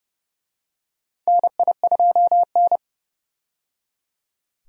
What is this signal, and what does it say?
Morse code, heard as a keyed tone, about 30 wpm, 700 Hz. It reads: NI2N
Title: USA Call Signs - 4 Characters - 30wpm